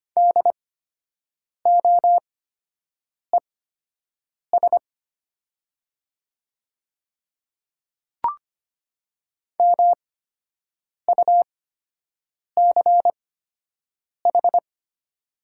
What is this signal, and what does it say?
Morse code, heard as a keyed tone, about 25 wpm, spaced as for 7 wpm, 700 Hz. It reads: DOES MUCH
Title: 8x Character Spacing - Top 200 Words - 25wpm